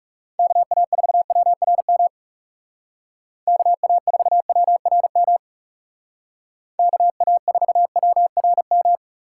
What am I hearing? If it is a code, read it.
Morse code, heard as a keyed tone, about 39 wpm, 700 Hz. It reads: KA4WRM KA4WRM KA4WRM